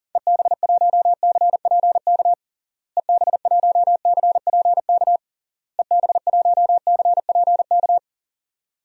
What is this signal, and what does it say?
Morse code, heard as a keyed tone, about 40 wpm, 700 Hz. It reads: EB1CPK EB1CPK EB1CPK
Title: DXCC-100 Call Signs - 6 Characters - Repeated 3x - 40wpm